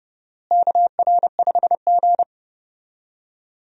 Morse code, heard as a keyed tone, about 29 wpm, 700 Hz. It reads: KR5G
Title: USA Call Signs - 4 Characters - Repeated 2x - 30wpm